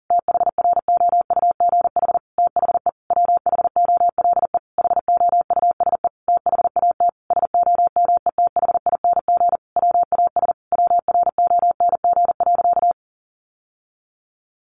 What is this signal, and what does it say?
Morse code, heard as a keyed tone, about 40 wpm, 700 Hz. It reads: THROUGH THE WHOLE HOUSE THAT SOMETHING WAS WRONG.
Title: Sentences from Top 1200 Words 40wpm